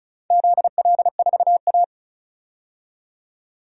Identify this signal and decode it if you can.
Morse code, heard as a keyed tone, about 36 wpm, 700 Hz. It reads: ZL4A
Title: DXCC-100 Call Signs - 4 Characters - 35wpm